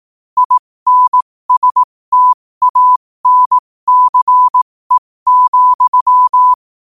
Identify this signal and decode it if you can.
Morse code, heard as a keyed tone, about 18 wpm, spaced as for 15 wpm, 1000 Hz. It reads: INSTANCE,
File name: qotdfiles / 15wpm